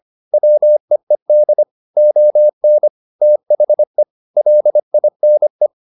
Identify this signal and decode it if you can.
Morse code, heard as a keyed tone, about 25 wpm, 600 Hz. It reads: WEED ON THE LINE
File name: the old man and the sea part 1 25wpm